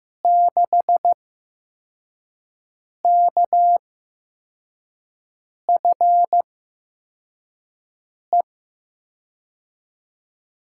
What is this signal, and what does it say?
Morse code, heard as a keyed tone, about 15 wpm, spaced as for 4 wpm, 700 Hz. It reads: 6KFE
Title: USA Call Signs - 5 Characters - 8x Character Spacing - 15wpm